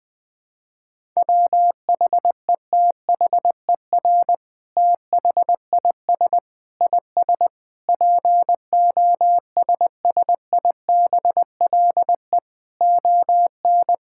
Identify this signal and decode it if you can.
Morse code, heard as a keyed tone, about 21 wpm, 700 Hz. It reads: WHETHER THIS IS POSSIBLE ON
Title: Sentences from Top 1000 Words - Review Part 5 - 20wpm